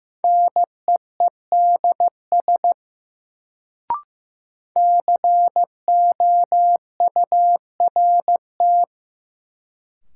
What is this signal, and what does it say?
Morse code, heard as a keyed tone, about 15 wpm, 700 Hz. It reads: NEEDS COURT
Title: Sentences from Top 1600 Words - New Word Drill Alt - 15wpm